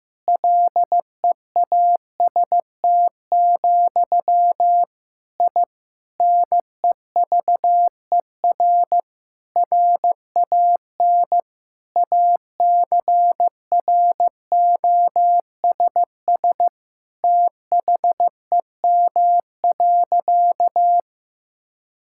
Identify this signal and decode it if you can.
Morse code, heard as a keyed tone, about 15 wpm, 700 Hz. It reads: LEAST, I NEVER RAN ACROSS THEM.
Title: Sentences from Top 1000 Words - Review Part 5 - 15wpm